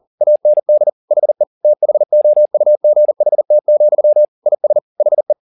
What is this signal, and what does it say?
Morse code, heard as a keyed tone, about 40 wpm, 600 Hz. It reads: AND HE THOUGHT, IS HE
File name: the old man and the sea 2 40wpm